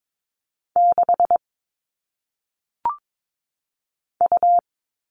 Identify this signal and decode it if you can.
Morse code, heard as a keyed tone, about 22 wpm, 700 Hz. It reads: 6 U
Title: All Cut Numbers - 1-Digit 22wpm